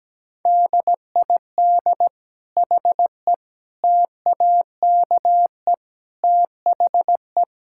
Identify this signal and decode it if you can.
Morse code, heard as a keyed tone, about 17 wpm, 700 Hz. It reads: DID HE TAKE THE
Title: Sentences from Top 1000 Words - Review Part 6 - 17wpm